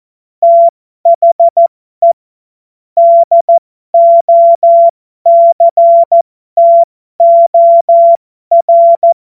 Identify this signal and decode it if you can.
Morse code, heard as a keyed tone, about 14 wpm, spaced as for 12 wpm, 680 Hz. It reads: THE DOCTOR
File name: Morsekurs / Witze